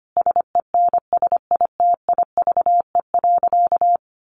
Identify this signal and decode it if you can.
Morse code, heard as a keyed tone, about 25 wpm, 700 Hz. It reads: SENSITIVE.
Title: Sentences from Top 1400 Words 25wpm